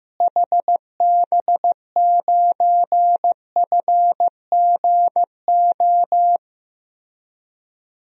Morse code, heard as a keyed tone, about 15 wpm, 700 Hz. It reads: HB9FGO
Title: DXCC-100 Call Signs - 6 Characters - Repeated 2x - 15wpm